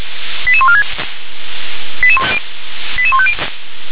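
Five-tone selective call (ZVEI), heard as quick runs of five tones, tones of about 70 ms. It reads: 88146 88150
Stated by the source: SDR-received ZVEI five-tone selective call, code 88740.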